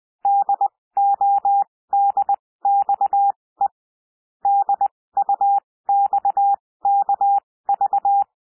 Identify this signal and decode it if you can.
Morse code, heard as a keyed tone, about 20 wpm, spaced as for 16 wpm, 800 Hz. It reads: DODXE DUXKV
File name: cwmp3 / L1 SOUND LETTERS 20 wpm